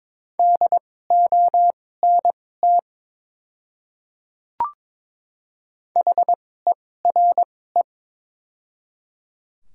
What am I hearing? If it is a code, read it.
Morse code, heard as a keyed tone, about 22 wpm, spaced as for 16 wpm, 700 Hz. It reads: DONT HERE
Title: 2x Character Spacing - Top 500 Words - 22wpm